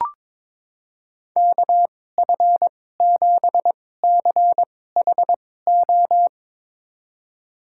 Morse code, heard as a keyed tone, about 22 wpm, spaced as for 16 wpm, 700 Hz. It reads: KF7CHO
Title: USA Call Signs - 6 Characters - 2x Character Spacing - 22wpm